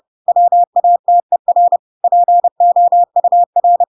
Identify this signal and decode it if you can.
Morse code, heard as a keyed tone, about 30 wpm, 700 Hz. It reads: WATER POUR
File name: Pinocchio / Pinocchio ch6 30 30 ik1hge